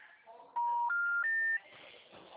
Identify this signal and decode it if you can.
Special information tone.
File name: out- -2041- - - .22608